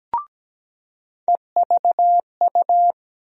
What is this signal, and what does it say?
Morse code, heard as a keyed tone, about 17 wpm, 700 Hz. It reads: EVU
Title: USA Call Sign Postfixes - 3 Letters 17wpm